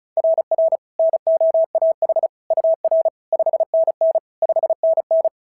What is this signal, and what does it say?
Morse code, heard as a keyed tone, about 36 wpm, 690 Hz. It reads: RR NOAH UR 5NN 5NN
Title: SOTA QSO Practice 35wpm